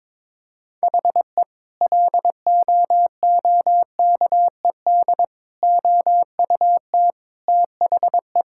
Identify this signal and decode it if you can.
Morse code, heard as a keyed tone, about 22 wpm, 700 Hz. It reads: HE LOOKED OUT THE
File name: practice / 22wpm / Sentences with 2-Word Phrases - 22wpm